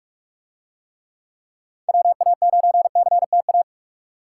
Morse code, heard as a keyed tone, about 40 wpm, 700 Hz. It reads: WA9CTA